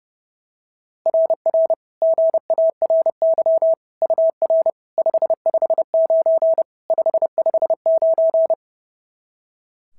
Morse code, heard as a keyed tone, about 31 wpm, 690 Hz. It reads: RR GARY UR 559 559
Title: SOTA QSO Practice 30wpm